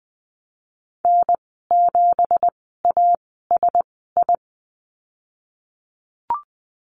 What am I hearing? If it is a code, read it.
Morse code, heard as a keyed tone, about 20 wpm, spaced as for 14 wpm, 700 Hz. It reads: N7ASI
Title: 173 Lesson 28g - Number 9 - Call Signs 20wpm